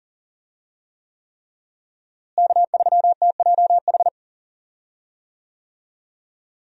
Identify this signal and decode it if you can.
Morse code, heard as a keyed tone, about 39 wpm, 700 Hz. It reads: K3TJH
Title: USA Call Signs - 5 Characters - Repeated 3x - 40wpm